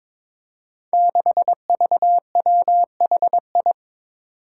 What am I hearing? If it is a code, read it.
Morse code, heard as a keyed tone, about 23 wpm, 700 Hz. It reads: 6VWHI
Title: Five Challenging Letter-Number Combinations 22wpm